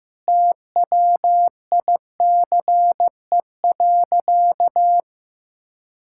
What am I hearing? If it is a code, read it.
Morse code, heard as a keyed tone, about 15 wpm, 700 Hz. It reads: TWICE.
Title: Sentences from Top 1600 Words 15wpm